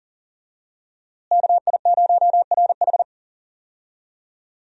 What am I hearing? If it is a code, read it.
Morse code, heard as a keyed tone, about 40 wpm, 700 Hz. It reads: KI0RH
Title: USA Call Signs - 5 Characters - Repeated 2x - 40wpm